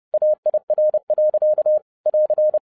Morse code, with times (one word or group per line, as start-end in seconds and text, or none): 0.1-1.8 AIR.
2.1-2.6 <AR>